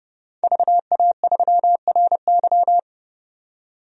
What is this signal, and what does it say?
Morse code, heard as a keyed tone, about 32 wpm, 700 Hz. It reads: VA3RY
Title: DXCC-100 Call Signs - 5 Characters - 30wpm